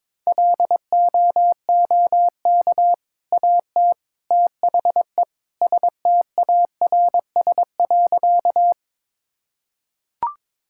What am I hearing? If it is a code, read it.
Morse code, heard as a keyed tone, about 22 wpm, 700 Hz. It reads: LOOK AT THE STARS.
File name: practice / 22wpm / Sentences with 3-Word Phrases - 22wpm